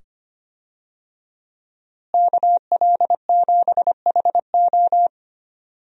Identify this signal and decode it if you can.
Morse code, heard as a keyed tone, about 25 wpm, 700 Hz. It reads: KL7HO
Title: DXCC-100 Call Signs - 5 Characters - Repeated 2x - 25wpm